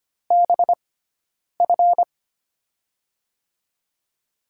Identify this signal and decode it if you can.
Morse code, heard as a keyed tone, about 25 wpm, spaced as for 9 wpm, 700 Hz. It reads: BF
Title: USA Call Signs - 5 Characters - 6x Character Spacing - 25wpm